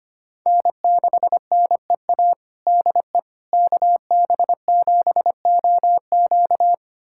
Morse code, heard as a keyed tone, about 25 wpm, 700 Hz. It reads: N6NEA DE KB7OQ
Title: USA Callsign DE Callsign - 5 Characters - 25wpm